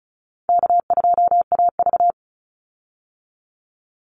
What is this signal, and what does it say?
Morse code, heard as a keyed tone, about 34 wpm, 700 Hz. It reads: K2AV
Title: Easy Call Signs Repeated 3x - 35wpm